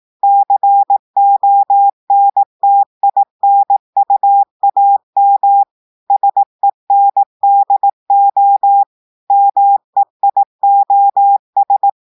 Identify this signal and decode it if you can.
Morse code, heard as a keyed tone, about 18 wpm, 800 Hz. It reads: CONTINUAM SENDO MEIOS
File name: portugais texte en clair 2 vitesse 18 mots minutes